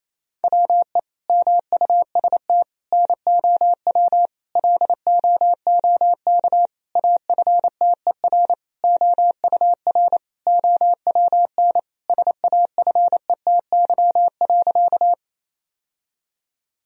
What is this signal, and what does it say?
Morse code, heard as a keyed tone, about 28 wpm, 700 Hz. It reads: WE MUST NOW LOOK AFTER OUR OWN SAFETY.
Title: Sentences from Top 1000 Words - Review Part 4 - 28wpm